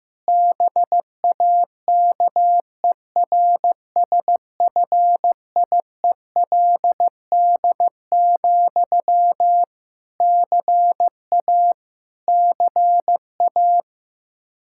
Morse code, heard as a keyed tone, about 15 wpm, 700 Hz. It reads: BAKERSFIELD, CA CA
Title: Practice QSO - Exchange 1 - Beginning - 15wpm